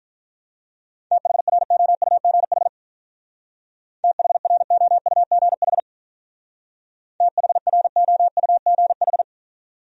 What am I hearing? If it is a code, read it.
Morse code, heard as a keyed tone, about 45 wpm, 700 Hz. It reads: THROUGH THROUGH THROUGH